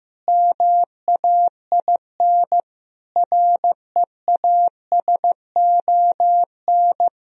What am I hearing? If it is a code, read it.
Morse code, heard as a keyed tone, about 15 wpm, 700 Hz. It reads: MAIN REASON
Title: Sentences from Top 1000 Words - Review Part 2 - 15wpm